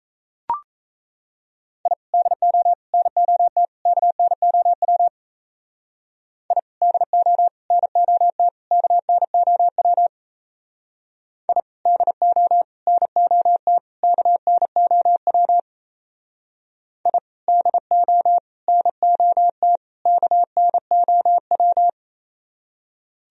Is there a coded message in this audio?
Yes: Morse code from 1.9 to 21.9 s, heard as a keyed tone, about 36 wpm, 700 Hz. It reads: I DO NOT KNOW I DO NOT KNOW I DO NOT KNOW I DO NOT KNOW